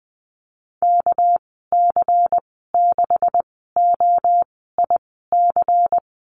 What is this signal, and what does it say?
Morse code, heard as a keyed tone, about 20 wpm, spaced as for 14 wpm, 700 Hz. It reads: KC6OIC
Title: 207 Lesson 33g - Letter j - Call Signs 20wpm